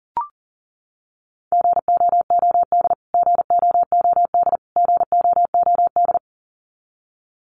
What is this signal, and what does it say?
Morse code, heard as a keyed tone, about 40 wpm, 700 Hz. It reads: GOOD GOOD GOOD
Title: Top 500 Words Repeated 40wpm